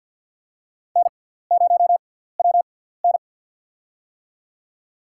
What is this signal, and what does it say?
Morse code, heard as a keyed tone, about 53 wpm, spaced as for 17 wpm, 700 Hz. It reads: N0WN
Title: USA Call Signs - 4 Characters - 6x Character Spacing - 50wpm